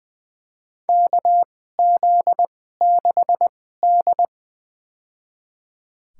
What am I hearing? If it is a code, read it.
Morse code, heard as a keyed tone, about 20 wpm, spaced as for 15 wpm, 700 Hz. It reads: KZ6D
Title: USA Call Signs - 4 Characters - 2x Character Spacing - 20wpm